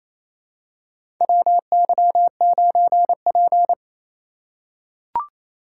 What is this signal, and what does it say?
Morse code, heard as a keyed tone, about 28 wpm, 700 Hz. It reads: WY9P
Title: USA Call Signs - 4 Characters - 3x Character Spacing - 28wpm